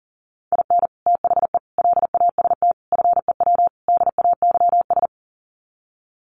Morse code, heard as a keyed tone, about 40 wpm, 700 Hz. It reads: IN THE LAST FEW DAYS
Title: Sets of 5 Words Encore2 40wpm